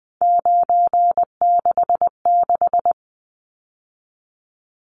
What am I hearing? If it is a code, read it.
Morse code, heard as a keyed tone, about 20 wpm, 700 Hz. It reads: 966
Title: Numbers - 3-Digit 20wpm